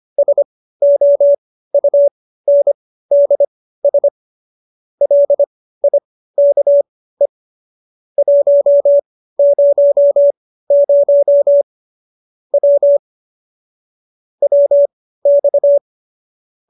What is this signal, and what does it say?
Morse code, heard as a keyed tone, about 25 wpm, spaced as for 15 wpm, 570 Hz. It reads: SOUNDS LIKE 100 W WX